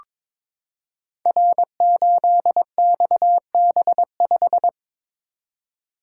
Morse code, heard as a keyed tone, about 22 wpm, 700 Hz. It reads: R8XB5
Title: Code Groups - 5 Letter:Numbers - 22wpm